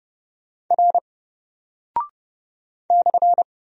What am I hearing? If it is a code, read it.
Morse code, heard as a keyed tone, about 28 wpm, 700 Hz. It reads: R /
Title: Single Character (Letter:Number:Punctuation:Prosign) 30wpm